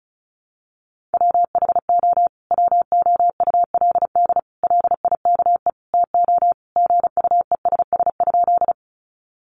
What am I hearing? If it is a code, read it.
Morse code, heard as a keyed tone, about 34 wpm, 700 Hz. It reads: WHO WOULD LIKE TO GUESS?
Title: Sentences from Top 1500 Words 35wpm